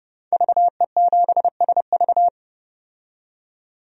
Morse code, heard as a keyed tone, about 31 wpm, 700 Hz. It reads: VE7SV
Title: DXCC-100 Call Signs - 5 Characters - 30wpm